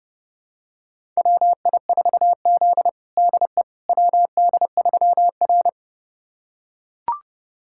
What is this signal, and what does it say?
Morse code, heard as a keyed tone, about 30 wpm, 700 Hz. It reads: WI4Z DE WD3R
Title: USA Callsign DE Callsign - 4 Characters - 30wpm